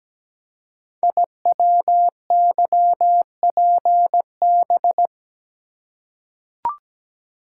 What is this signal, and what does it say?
Morse code, heard as a keyed tone, about 17 wpm, 700 Hz. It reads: IWYPB
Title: Code Groups - 5 Letters - 17wpm